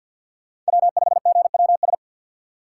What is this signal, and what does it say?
Morse code, heard as a keyed tone, about 46 wpm, 700 Hz. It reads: W5GWS
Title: USA Call Signs - 5 Characters - 4x Character Spacing - 50wpm